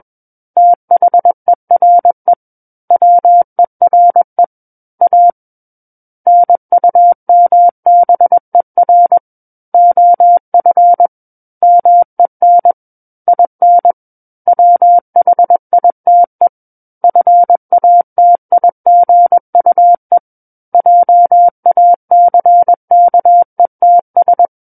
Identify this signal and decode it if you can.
Morse code, heard as a keyed tone, about 21 wpm, 700 Hz. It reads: THERE WERE A NUMBER OF MEN IN WHITE FATIGUE JACKETS
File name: Book One / 12-Chapter 12 - 21 WPM